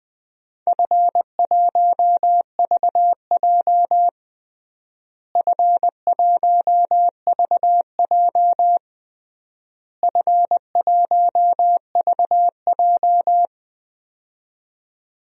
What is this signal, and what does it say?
Morse code, heard as a keyed tone, about 20 wpm, 700 Hz. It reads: F1VJ F1VJ F1VJ
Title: DXCC-100 Call Signs - 4 Characters - Repeated 3x - 20wpm